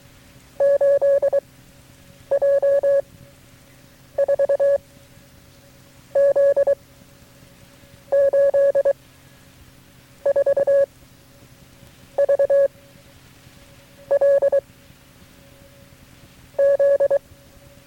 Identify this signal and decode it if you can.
Morse code, heard as a keyed tone, about 23 wpm, 560 Hz. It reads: 8 J 4 Z 8 4 V L Z
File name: FreeFiles / CW-practice / KB6MT-tape3-side1